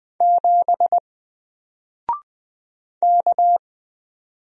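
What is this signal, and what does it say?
Morse code, heard as a keyed tone, about 20 wpm, 700 Hz. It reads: 7 K
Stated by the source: Single letter and number Morse drill at 20 wpm, from N4LCD.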